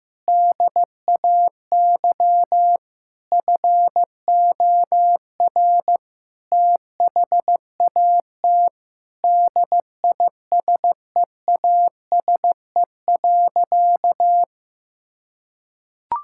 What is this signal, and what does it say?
Morse code, heard as a keyed tone, about 15 wpm, 700 Hz. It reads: DAY FOR THAT DISEASE.
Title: Sentences from Top 1000 Words - Review Part 6 - 15wpm